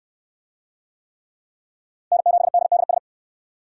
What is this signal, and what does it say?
Morse code, heard as a keyed tone, about 69 wpm, 700 Hz. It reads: N7DDS